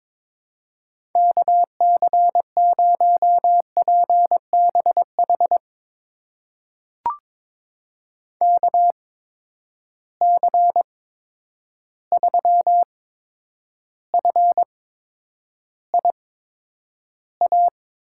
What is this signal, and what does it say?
Morse code, heard as a keyed tone, about 22 wpm, spaced as for 9 wpm, 700 Hz. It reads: KC0PBH KC3FIA
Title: USA Call Signs - 6 Characters - 8x Character Spacing - 22wpm